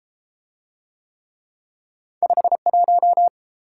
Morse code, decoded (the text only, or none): 51